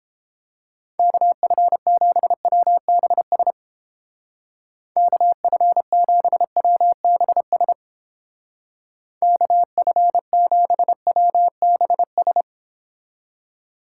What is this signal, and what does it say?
Morse code, heard as a keyed tone, about 29 wpm, 700 Hz. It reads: KF7WBS KF7WBS KF7WBS